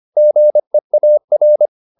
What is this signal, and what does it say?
Morse code, heard as a keyed tone, about 25 wpm, 600 Hz. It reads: GEAR
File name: the old man and the sea part 1 25wpm